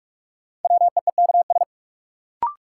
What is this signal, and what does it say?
Morse code, heard as a keyed tone, about 41 wpm, 700 Hz. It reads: WEEKS
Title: Sets of 5 Words - New Word Drill - 45wpm